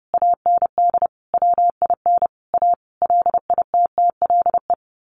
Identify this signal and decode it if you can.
Morse code, heard as a keyed tone, about 30 wpm, 700 Hz. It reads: AND WIN A LITTLE
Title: Sentences from Top 1500 Words 30wpm